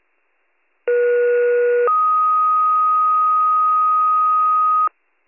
Two-tone page, heard as a one-second tone, then a three-second tone, about 480 then 1250 Hz. It is heard from 0.9 to 4.9 s.